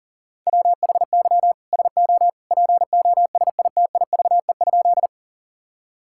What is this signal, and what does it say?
Morse code, heard as a keyed tone, about 40 wpm, 700 Hz. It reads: WHY SO POSITIVE?
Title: Sentences from Top 1000 Words - Review Part 5 - 40wpm